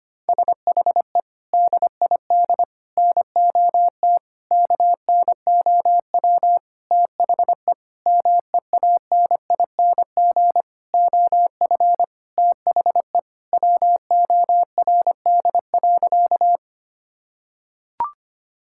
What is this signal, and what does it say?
Morse code, heard as a keyed tone, about 25 wpm, 700 Hz. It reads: SHE DID NOT KNOW THE MEANING OF THE WORD.
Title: Sentences from Top 1000 Words - Review Part 6 - 25wpm